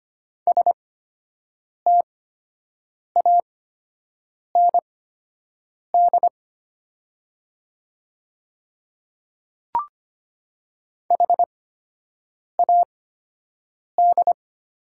Morse code, heard as a keyed tone, about 25 wpm, spaced as for 7 wpm, 700 Hz. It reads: STAND HAD
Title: 8x Character Spacing - Top 300 Words - 25wpm